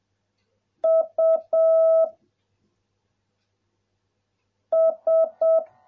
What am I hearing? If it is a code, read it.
Morse code, heard as a keyed tone, about 7 wpm, 660 Hz. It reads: U S